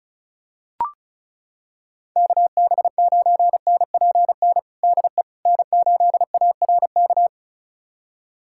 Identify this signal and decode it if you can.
Morse code, heard as a keyed tone, about 36 wpm, 700 Hz. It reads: KB9NPN DE N8ARK
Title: USA Callsign DE Callsign 35wpm